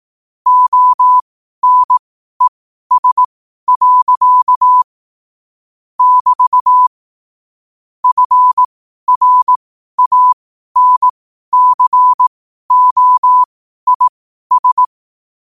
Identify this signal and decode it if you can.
Morse code, heard as a keyed tone, about 18 wpm, spaced as for 12 wpm, 1000 Hz. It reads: ONES. <BT> FRANCOIS
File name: qotdfiles / 12wpm